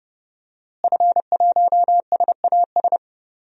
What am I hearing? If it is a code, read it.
Morse code, heard as a keyed tone, about 30 wpm, 700 Hz. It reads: F1SAS
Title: DXCC-100 Call Signs - 5 Characters - Repeated 3x - 30wpm